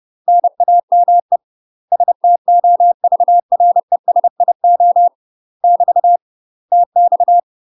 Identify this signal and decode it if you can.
Morse code, heard as a keyed tone, about 30 wpm, 700 Hz. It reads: NAME STOVRESIO <BT> TX
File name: UK MASTER QSOs TRAINING / 30WPM / UK MASTER QSOs 0043